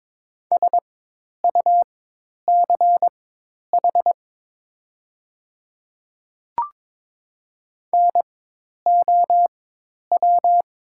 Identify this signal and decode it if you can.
Morse code, heard as a keyed tone, about 22 wpm, spaced as for 10 wpm, 700 Hz. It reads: SUCH NOW
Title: 4x Character Spacing - Top 200 Words - 22wpm